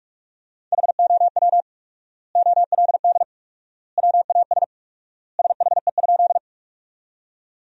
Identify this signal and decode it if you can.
Morse code, heard as a keyed tone, about 46 wpm, 700 Hz. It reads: HOW OLD WAS SHE?
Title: 4x Word-Spacing - Sentences from Top 200 Words - 45wpm